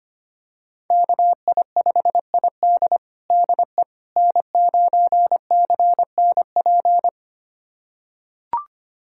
Morse code, heard as a keyed tone, about 25 wpm, 700 Hz. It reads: KI5ID DE N9CNP
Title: USA Callsign DE Callsign - 5 Characters - 25wpm